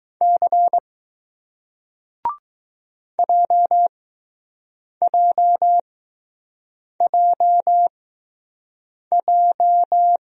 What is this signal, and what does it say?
Morse code, heard as a keyed tone, about 20 wpm, spaced as for 6 wpm, 700 Hz. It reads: C JJJJ